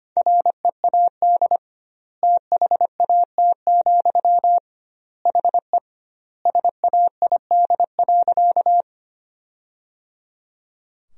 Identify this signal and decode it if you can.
Morse code, heard as a keyed tone, about 25 wpm, 700 Hz. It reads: READ THAT, HE SAID.
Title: 2x Word-Spacing - Sentences from Top 200 Words - 25wpm